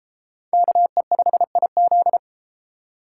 Morse code, heard as a keyed tone, about 30 wpm, 700 Hz. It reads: KE5IZ